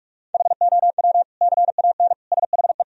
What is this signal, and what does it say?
Morse code, heard as a keyed tone, about 46 wpm, 700 Hz. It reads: HOW CAN SHE
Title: Sentences from Top 1000 Words - Review Part 4 - 45wpm